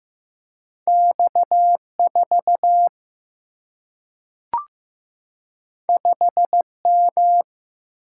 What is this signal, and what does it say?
Morse code, heard as a keyed tone, about 15 wpm, 700 Hz. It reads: X4 5M
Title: Single-Letter Single-Number 15wpm